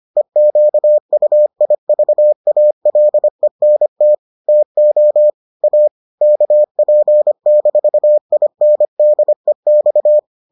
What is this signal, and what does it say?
Morse code, heard as a keyed tone, about 25 wpm, 600 Hz. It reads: EQUIVALENT TO A KP-INDEX